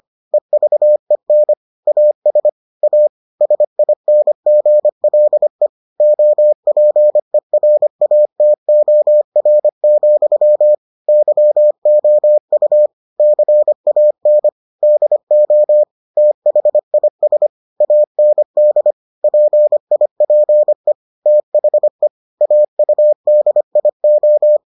Morse code, heard as a keyed tone, about 25 wpm, 600 Hz. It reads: EVEN AS A SINGLE OPERATOR, YOU CAN DO THIS AND PIPE THE AUDIO